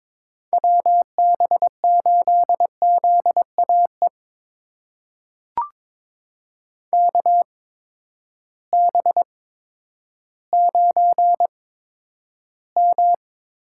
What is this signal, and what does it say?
Morse code, heard as a keyed tone, about 22 wpm, spaced as for 11 wpm, 700 Hz. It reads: WB8ZAE KB9M
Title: USA Call Signs - 6 Characters - 8x Character Spacing - 22wpm